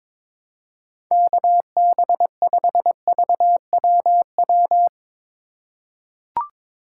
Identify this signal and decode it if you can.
Morse code, heard as a keyed tone, about 22 wpm, 700 Hz. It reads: KB5VWW
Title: USA Call Signs - 2x Character Spacing - 22wpm